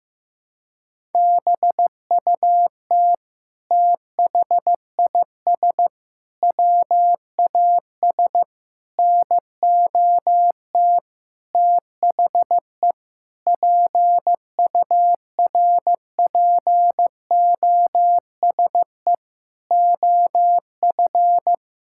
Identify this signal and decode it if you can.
Morse code, heard as a keyed tone, about 16 wpm, 700 Hz. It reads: BUT THIS WAS NOT THE PURPOSE OF
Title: Sentences from Top 1000 Words - Review Part 6 - 15wpm